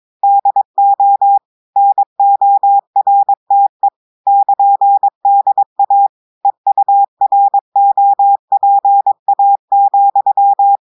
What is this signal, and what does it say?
Morse code, heard as a keyed tone, about 22 wpm, 800 Hz. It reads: DO NORTE <KN>DA EUROPA,
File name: audios / portugais / portugais texte en clair 1 vitesse 22 mots minutes